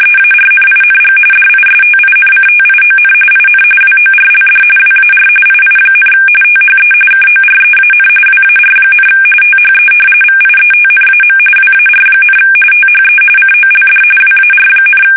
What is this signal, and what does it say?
RTTY, about 45.45 baud. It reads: EARNING DISTINCTION IN WORLD WAR II, THE KOREAN WAR, THE VIETNAM WAR, THE COLD WAR,
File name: fsk850 / battleshipnj